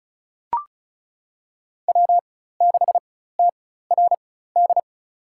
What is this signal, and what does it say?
Morse code, heard as a keyed tone, about 36 wpm, spaced as for 16 wpm, 700 Hz. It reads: W6TRD
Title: USA Call Signs - 5 Characters - 4x Character Spacing - 35wpm